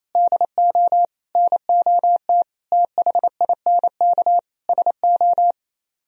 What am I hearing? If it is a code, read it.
Morse code, heard as a keyed tone, about 34 wpm, 700 Hz. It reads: DO NOT THINK SO
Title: Speed-Racing - Sets of 5 Words 28wpm